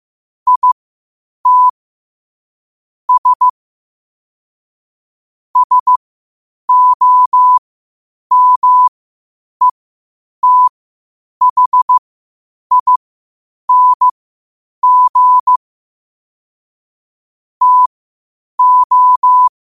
Morse code, heard as a keyed tone, about 15 wpm, spaced as for 8 wpm, 1000 Hz. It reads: IT S SOMETHING TO
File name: qotdfiles / 8wpm